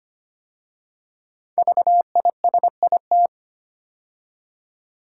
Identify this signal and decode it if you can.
Morse code, heard as a keyed tone, about 25 wpm, 700 Hz. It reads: VISIT
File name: 5 Letter Words - First 3 Letters Given - 25wpm